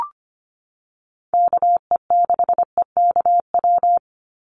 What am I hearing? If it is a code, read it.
Morse code, heard as a keyed tone, about 25 wpm, 700 Hz. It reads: KE6EKW
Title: USA Call Signs 25wpm